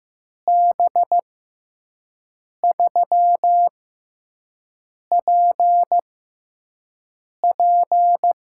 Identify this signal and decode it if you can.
Morse code, heard as a keyed tone, about 15 wpm, spaced as for 5 wpm, 700 Hz. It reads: B3PP
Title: USA Call Signs - 6 Characters - 6x Character Spacing - 15wpm